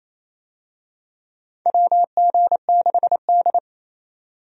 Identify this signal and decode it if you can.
Morse code, heard as a keyed tone, about 28 wpm, 700 Hz. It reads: WG6D
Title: USA Call Signs - 4 Characters - 28wpm